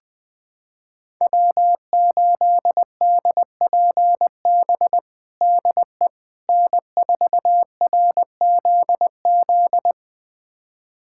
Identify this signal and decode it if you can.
Morse code, heard as a keyed tone, about 20 wpm, 700 Hz. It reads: W8DPB DE N4RZZ
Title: USA Callsign DE Callsign - 5 Characters - 20wpm